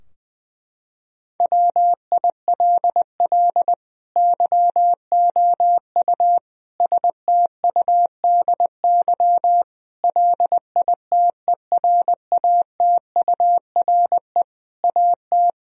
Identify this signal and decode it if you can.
Morse code, heard as a keyed tone, about 20 wpm, 700 Hz. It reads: WILL YOU STUDY LITERATURE AT
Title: Sentences from Top 1000 Words - Review Part 3 - 20wpm